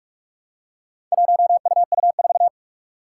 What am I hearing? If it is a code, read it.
Morse code, heard as a keyed tone, about 44 wpm, 700 Hz. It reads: 1UU4
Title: Four Challenging Letter-Number Combinations 45wpm